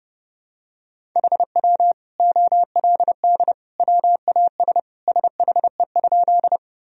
Morse code, heard as a keyed tone, about 32 wpm, 700 Hz. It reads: HW OLD WAS SHE?
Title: Sentences from Top 200 Words and Abbreviations 30wpm